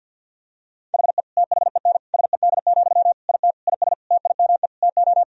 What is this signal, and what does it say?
Morse code, heard as a keyed tone, about 50 wpm, 700 Hz. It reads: HE THEN SED, IT IS TIME TO
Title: Sentences from Top 300 Words and Abbreviations 50wpm